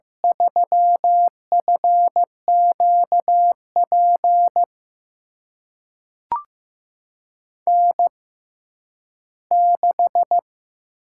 Morse code, heard as a keyed tone, about 15 wpm, spaced as for 10 wpm, 700 Hz. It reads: 3FQP N6
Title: USA Call Signs - 5 Characters - 6x Character Spacing - 15wpm